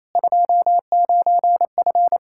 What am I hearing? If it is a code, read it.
Morse code, heard as a keyed tone, about 28 wpm, 700 Hz. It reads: 29F
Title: K1USN SST QSO Practice 28wpm